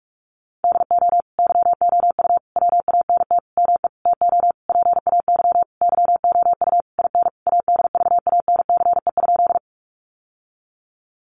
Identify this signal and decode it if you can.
Morse code, heard as a keyed tone, about 44 wpm, 700 Hz. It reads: DO YOU WANT ME TO PAY YOU IN ADVANCE?
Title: Sentences from Top 1400 Words 45wpm